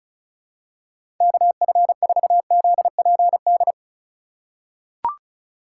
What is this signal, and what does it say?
Morse code, heard as a keyed tone, about 36 wpm, 700 Hz. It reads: KF4ZPD
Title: USA Call Signs - 6 Characters - 3x Character Spacing - 35wpm